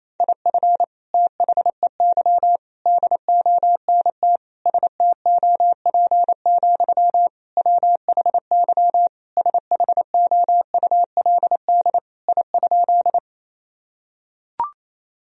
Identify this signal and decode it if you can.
Morse code, heard as a keyed tone, about 28 wpm, 700 Hz. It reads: IF THEY DONT STOP, WHY SHOULD I?
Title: Sentences from Top 1000 Words - Review Part 1 - 28wpm